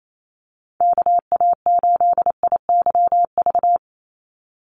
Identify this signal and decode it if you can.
Morse code, heard as a keyed tone, about 28 wpm, 700 Hz. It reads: KA8IYV
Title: USA Call Signs 28wpm